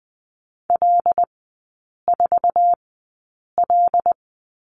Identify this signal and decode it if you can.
Morse code, heard as a keyed tone, about 20 wpm, spaced as for 8 wpm, 700 Hz. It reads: L4L
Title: 074 Lesson 13e - Letter L - Three Characters 20wpm